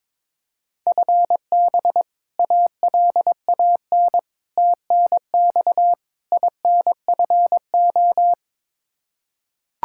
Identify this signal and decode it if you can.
Morse code, heard as a keyed tone, about 21 wpm, 700 Hz. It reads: FB ALAN TNX INFO
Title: Practice QSO - Exchange 3 - More Info - 22wpm